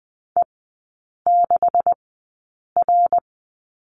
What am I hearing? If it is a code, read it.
Morse code, heard as a keyed tone, about 20 wpm, spaced as for 8 wpm, 700 Hz. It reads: E6R
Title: 127 Lesson 21e - Number 6 - Three Characters 20wpm